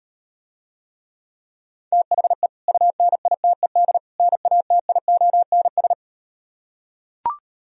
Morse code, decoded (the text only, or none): THE UNITED NATIONS